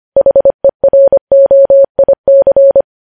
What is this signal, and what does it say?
Morse code, heard as a keyed tone, about 25 wpm, 550 Hz. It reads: HEROIC
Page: News Headlines in Morse Code at 25 WPM